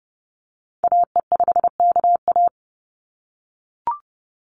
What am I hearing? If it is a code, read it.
Morse code, heard as a keyed tone, about 30 wpm, 700 Hz. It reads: AE5KA
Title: NA SOTA Activator Callsigns 30wpm